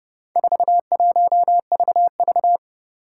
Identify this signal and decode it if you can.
Morse code, heard as a keyed tone, about 30 wpm, 700 Hz. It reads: 41VV
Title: Four Challenging Letter-Number Combinations 30wpm